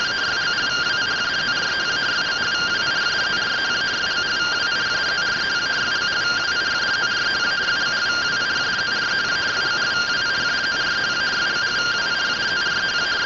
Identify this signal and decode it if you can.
Two-tone teleprinter signal, about 170 Hz shift.